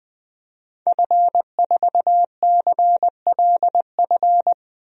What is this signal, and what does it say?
Morse code, heard as a keyed tone, about 20 wpm, 700 Hz. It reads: F4CLF
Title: DXCC-100 Call Signs - 5 Characters - 20wpm